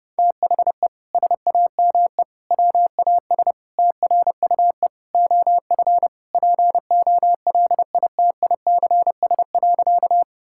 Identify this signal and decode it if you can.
Morse code, heard as a keyed tone, about 32 wpm, 700 Hz. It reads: THE SAME WAS TRUE OF POLITICS.
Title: Sentences from Top 1000 Words - Review Part 6 - 30wpm